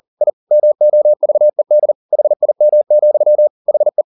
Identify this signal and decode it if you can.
Morse code, heard as a keyed tone, about 40 wpm, 600 Hz. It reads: I MOVED HIM, HE
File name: the old man and the sea 2 40wpm